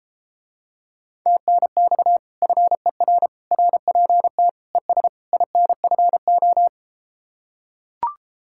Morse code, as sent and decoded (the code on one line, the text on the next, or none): - -. -..- / ..-. .-. / .-. .--. - / . ... / .. -. ..-. ---
TNX FER RPT ES INFO